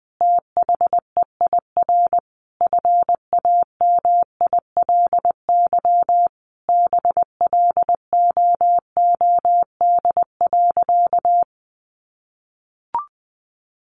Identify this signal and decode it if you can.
Morse code, heard as a keyed tone, about 19 wpm, 700 Hz. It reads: THEIR FAMILY BLOOD.
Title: Sentences from Top 1200 Words 20wpm